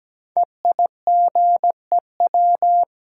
Morse code, heard as a keyed tone, about 17 wpm, 700 Hz. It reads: EIGEW
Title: Code Groups - 5 Letters - 17wpm